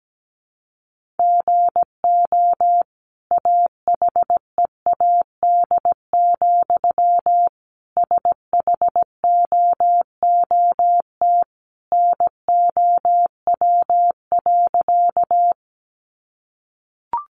Morse code, heard as a keyed tone, about 17 wpm, 700 Hz. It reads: GO AHEAD, SHOOT NOW.
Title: Sentences from Top 1200 Words 17wpm